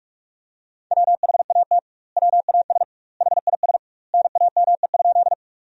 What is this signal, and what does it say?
Morse code, heard as a keyed tone, about 46 wpm, 700 Hz. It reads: WHAT WAS HIS NAME?
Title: 2x Word-Spacing - Sentences from Top 200 Words - 45wpm